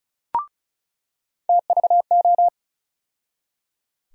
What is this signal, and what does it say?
Morse code, heard as a keyed tone, about 34 wpm, 700 Hz. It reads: TVO
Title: USA Call Sign Postfixes - 3 Letters 35wpm